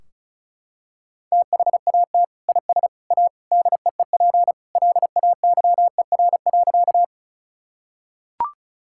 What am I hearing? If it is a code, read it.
Morse code, heard as a keyed tone, about 34 wpm, 700 Hz. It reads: THAT IS A DEEP LAYER.
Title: Sentences from Top 1600 Words 35wpm